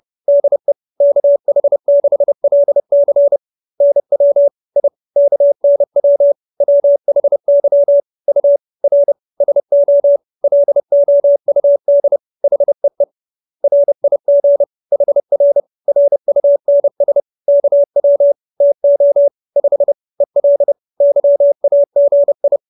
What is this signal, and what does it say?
Morse code, heard as a keyed tone, about 30 wpm, 570 Hz. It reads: DE KH6LC NW I KNW WHY U R SO LOUD HEE RIG HR RUNS KW TO 5 EL YAGI
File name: QSO311 30